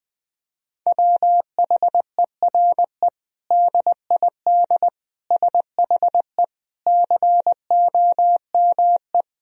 Morse code, heard as a keyed tone, about 20 wpm, 700 Hz. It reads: WHERE DID SHE COME
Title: Sentences from Top 200 Words and Abbreviations 20wpm